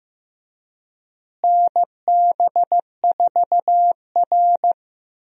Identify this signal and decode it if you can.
Morse code, heard as a keyed tone, about 15 wpm, 700 Hz. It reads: NB4R
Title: USA Call Signs - 4 Characters - Repeated 2x - 15wpm